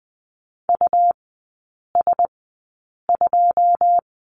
Morse code, heard as a keyed tone, about 20 wpm, spaced as for 8 wpm, 700 Hz. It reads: US2
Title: 107 Lesson 18e - Letter m - Three Characters 20wpm